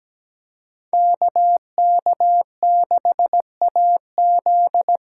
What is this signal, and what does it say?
Morse code, heard as a keyed tone, about 17 wpm, 700 Hz. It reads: KK6AZ
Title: USA Callsign DE Callsign 17wpm